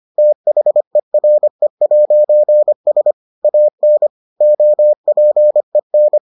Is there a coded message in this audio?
Yes: Morse code from 0.2 to 6.2 s, heard as a keyed tone, about 25 wpm, 600 Hz. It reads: THERE'S AN OPEN